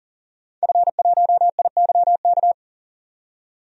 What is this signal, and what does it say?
Morse code, heard as a keyed tone, about 40 wpm, 700 Hz. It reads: F1IYK
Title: DXCC-100 Call Signs - 5 Characters - 40wpm